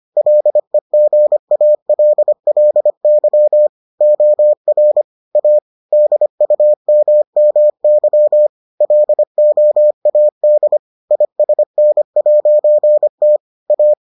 Morse code, heard as a keyed tone, about 25 wpm, 600 Hz. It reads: LEGALLY OR A DUMMY LOAD ISN'T A